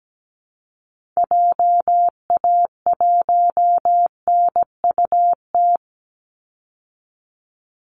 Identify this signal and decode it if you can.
Morse code, heard as a keyed tone, about 17 wpm, 700 Hz. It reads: JA1NUT
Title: Call Signs Repeated 2x - 17wpm